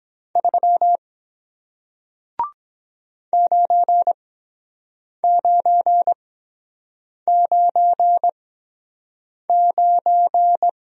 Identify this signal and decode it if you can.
Morse code, heard as a keyed tone, about 23 wpm, spaced as for 7 wpm, 700 Hz. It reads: 3 9999